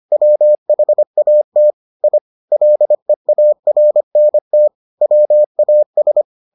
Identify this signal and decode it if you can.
Morse code, heard as a keyed tone, about 25 wpm, 600 Hz. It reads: WHAT I LEARNT WAS